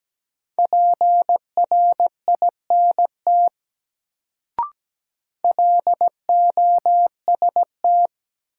Morse code, heard as a keyed tone, about 17 wpm, 700 Hz. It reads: PRINT LOST
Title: Sentences from Top 1400 Words - New Word Drill - 17wpm